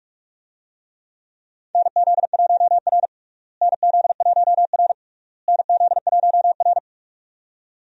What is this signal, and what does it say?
Morse code, heard as a keyed tone, about 46 wpm, 700 Hz. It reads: NZ1R NZ1R NZ1R
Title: USA Call Signs - 4 Characters - Repeated 3x - 45wpm